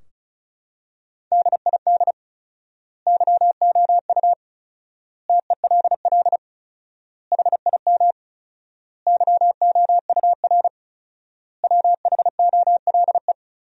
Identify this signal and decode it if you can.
Morse code, heard as a keyed tone, about 36 wpm, 700 Hz. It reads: DID YOU TELL HIM YOUR WHOLE
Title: 4x Word-Spacing - Sentences from Top 400 Words - 35wpm